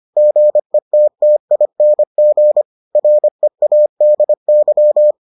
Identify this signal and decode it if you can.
Morse code, heard as a keyed tone, about 25 wpm, 600 Hz. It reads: GETTING READY